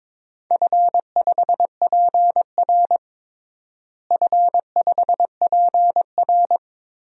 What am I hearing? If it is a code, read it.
Morse code, heard as a keyed tone, about 22 wpm, 700 Hz. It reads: F5PR F5PR
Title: DXCC-100 Call Signs - 4 Characters - Repeated 3x - 22wpm